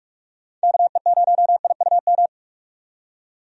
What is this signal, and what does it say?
Morse code, heard as a keyed tone, about 45 wpm, 700 Hz. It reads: KE0IUM
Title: USA Call Signs - 6 Characters - 45wpm